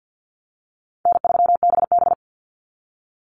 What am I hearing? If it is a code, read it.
Morse code, heard as a keyed tone, about 50 wpm, 700 Hz. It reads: N3BB
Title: Easy Call Signs Repeated 3x - 50wpm